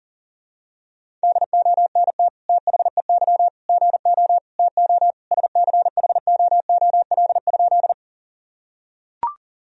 Morse code, heard as a keyed tone, about 37 wpm, 700 Hz. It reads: DONT THEY GO TO SCHOOL?
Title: Sentences from Top 1000 Words - Review Part 1 - 40wpm